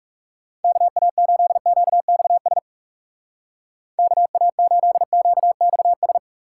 Morse code, heard as a keyed tone, about 35 wpm, 700 Hz. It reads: KA8QXS KA8QXS
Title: Speed Racing - USA 2x3 Call Signs 30wpm